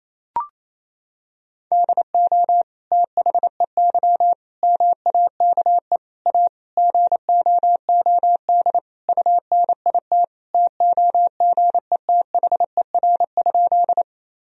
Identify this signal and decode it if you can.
Morse code, heard as a keyed tone, about 28 wpm, 700 Hz. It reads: DO THEY MAKE A GOOD UNIT TOGETHER?
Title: Sentences from Top 1000 Words - Review Part 2 - 28wpm